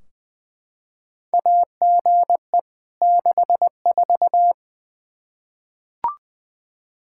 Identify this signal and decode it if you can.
Morse code, heard as a keyed tone, about 20 wpm, 700 Hz. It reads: AGE 64
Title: Practice QSO - Exchange 3 - More Info - 20wpm